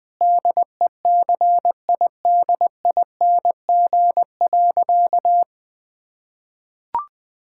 Morse code, heard as a keyed tone, about 21 wpm, 700 Hz. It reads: DECIDING.
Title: Difficult Sentences with Hints - 20wpm